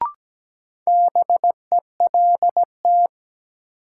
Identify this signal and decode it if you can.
Morse code, heard as a keyed tone, about 17 wpm, 700 Hz. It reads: BELT
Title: Sentences from Top 1400 Words - New Word Drill - 17wpm